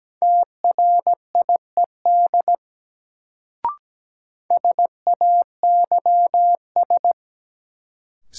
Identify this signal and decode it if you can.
Morse code, heard as a keyed tone, about 17 wpm, 700 Hz. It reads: TRIED SAYS
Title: Sets of 2 Words - New Word Drill - 17wpm